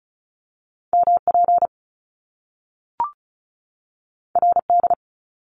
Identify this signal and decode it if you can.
Morse code, heard as a keyed tone, about 36 wpm, 700 Hz. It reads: MP RD
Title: Top Two Letter Combinations in Words 35wpm